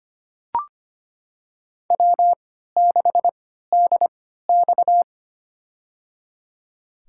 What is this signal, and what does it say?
Morse code, heard as a keyed tone, about 25 wpm, spaced as for 14 wpm, 700 Hz. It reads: W6DX
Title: USA Call Signs - 4 Characters - 3x Character Spacing - 25wpm